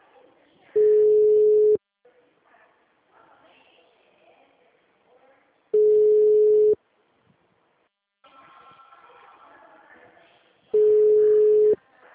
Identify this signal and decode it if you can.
Ringback tone (Europe, 425 Hz), with speech.